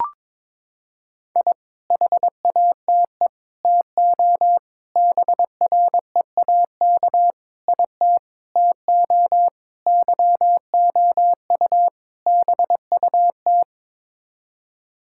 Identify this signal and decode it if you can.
Morse code, heard as a keyed tone, about 21 wpm, 700 Hz. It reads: I HATE TO BREAK IT TO YOU BUT
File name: practice / 22wpm / Sets of 8 Words - 22wpm